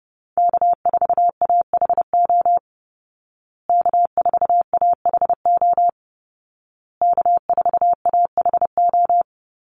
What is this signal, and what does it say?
Morse code, heard as a keyed tone, about 30 wpm, 700 Hz. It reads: K4AHO K4AHO K4AHO
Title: Call Signs Repeated 2x - 30wpm